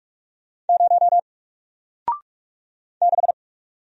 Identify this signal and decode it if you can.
Morse code, heard as a keyed tone, about 45 wpm, 700 Hz. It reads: 0 6
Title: Single Character (Letter:Number:Punctuation:Prosign) Alt 45wpm